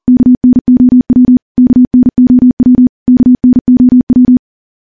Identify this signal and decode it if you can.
Morse code, heard as a keyed tone, about 40 wpm, 250 Hz. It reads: KNOW KNOW KNOW
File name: morse / morse-training-40WPM